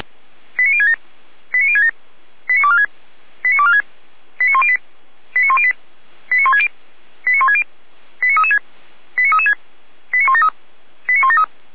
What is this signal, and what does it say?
Five-tone selective call (ZVEI), heard as quick runs of five tones, tones of about 70 ms. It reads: 89067 89067 89246 89246 89108 89108 89160 89160 89336 89336 89173 89173
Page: /455/SDR/ZVEI/89078/2025.08.13/